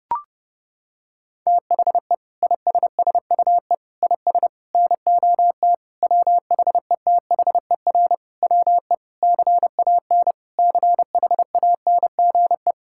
Morse code, heard as a keyed tone, about 32 wpm, 700 Hz. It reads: THE ISSUE IS NOT WHETHER WE CAN CHANGE
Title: Sentences from Top 1000 Words - Review Part 6 - 30wpm